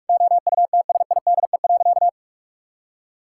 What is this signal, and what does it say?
Morse code, heard as a keyed tone, about 46 wpm, 700 Hz. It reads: OUTSIDE.
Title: Sentences from Top 1000 Words - Review Part 6 - 45wpm